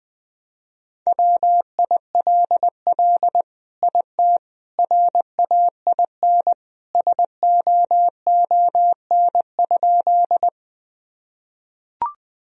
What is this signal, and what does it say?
Morse code, heard as a keyed tone, about 20 wpm, 700 Hz. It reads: WILL IT RAIN SOON?
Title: Sentences from Top 1000 Words - Review Part 1 - 20wpm